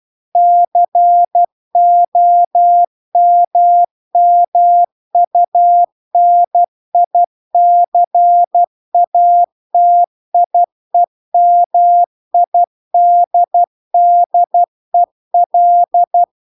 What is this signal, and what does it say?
Morse code, heard as a keyed tone, about 12 wpm, 700 Hz. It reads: COMMUNICATIEMIDDEL
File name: cw-cursus-VERON-a03